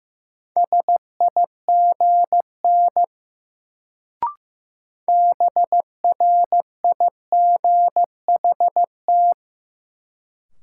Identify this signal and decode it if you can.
Morse code, heard as a keyed tone, about 15 wpm, 700 Hz. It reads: SIGN BRIGHT
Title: Sentences from Top 1000 Words - New Word Drill - 15wpm